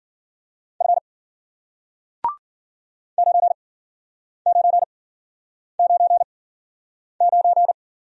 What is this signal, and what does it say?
Morse code, heard as a keyed tone, about 51 wpm, spaced as for 9 wpm, 700 Hz. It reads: F 9999